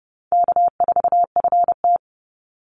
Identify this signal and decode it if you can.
Morse code, heard as a keyed tone, about 30 wpm, 700 Hz. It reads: K4FT
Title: Call Signs - 30wpm